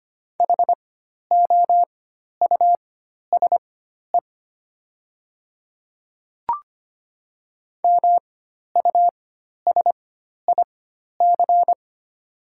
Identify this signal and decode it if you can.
Morse code, heard as a keyed tone, about 25 wpm, spaced as for 12 wpm, 700 Hz. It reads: HOUSE MUSIC
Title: 4x Character Spacing - Top 300 Words - 25wpm